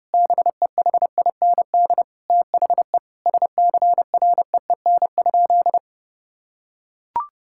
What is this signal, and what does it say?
Morse code, heard as a keyed tone, about 32 wpm, 700 Hz. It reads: BEHIND THE SCREEN?
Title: Sentences from Top 1000 Words - Review Part 6 - 30wpm